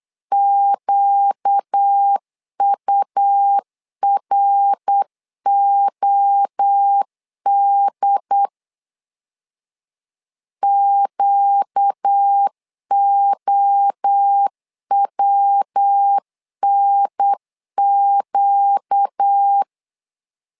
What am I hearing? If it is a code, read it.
Morse code, heard as a keyed tone, about 8 wpm, 800 Hz. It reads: QUROD QOWNQ
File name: LEZIONI / 4 Q 8wpm 2732